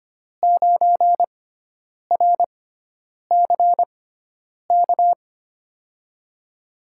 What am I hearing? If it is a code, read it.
Morse code, heard as a keyed tone, about 25 wpm, spaced as for 9 wpm, 700 Hz. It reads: 9RCK
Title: USA Call Signs - 6 Characters - 6x Character Spacing - 25wpm